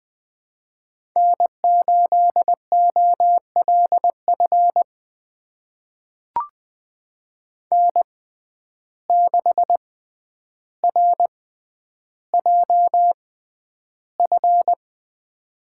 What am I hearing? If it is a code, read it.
Morse code, heard as a keyed tone, about 20 wpm, spaced as for 10 wpm, 700 Hz. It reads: N8OLF N6RJF
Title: USA Call Signs - 5 Characters - 6x Character Spacing - 20wpm